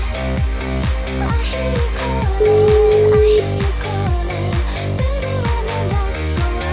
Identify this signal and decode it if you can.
Ringback tone (Europe, 425 Hz), with noise.